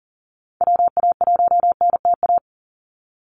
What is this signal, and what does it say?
Morse code, heard as a keyed tone, about 37 wpm, 700 Hz. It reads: WA1NTA
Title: Call Signs Repeated 3x - 40wpm